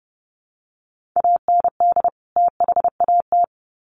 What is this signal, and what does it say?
Morse code, heard as a keyed tone, about 30 wpm, 700 Hz. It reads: AND THAT
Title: Sets of 2 Words 30wpm